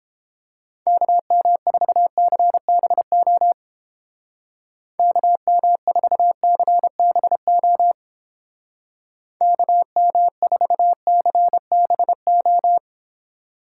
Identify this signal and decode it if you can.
Morse code, heard as a keyed tone, about 31 wpm, 700 Hz. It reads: KM4CBO KM4CBO KM4CBO